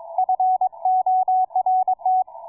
Morse code, heard as a keyed tone, about 22 wpm, 740 Hz, with radio noise. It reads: F ORT